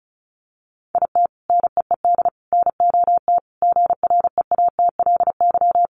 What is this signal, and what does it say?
Morse code, heard as a keyed tone, about 34 wpm, 700 Hz. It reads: IT NEED NOT GREATLY
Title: Sentences from Top 1400 Words 35wpm